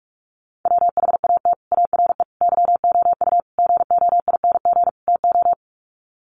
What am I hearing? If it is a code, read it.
Morse code, heard as a keyed tone, about 44 wpm, 700 Hz. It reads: WHAT ARE YOU GOING TO
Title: Sets of 5 Words 45wpm